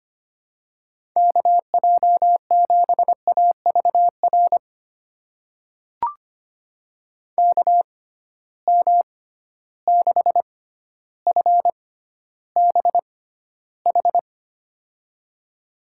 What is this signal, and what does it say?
Morse code, heard as a keyed tone, about 25 wpm, spaced as for 13 wpm, 700 Hz. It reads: KJ7AVR KM6FBH
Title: USA Call Signs - 6 Characters - 6x Character Spacing - 25wpm